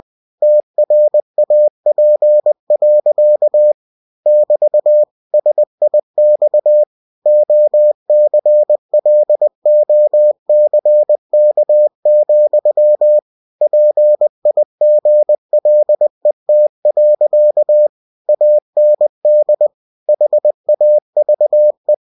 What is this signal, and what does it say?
Morse code, heard as a keyed tone, about 20 wpm, 600 Hz. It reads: TRAP. <BT> SIX OCLOCK, PIGLET. AND HAVE
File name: cw-books / winnie-the-pooh / winnie-20wpm-page-0038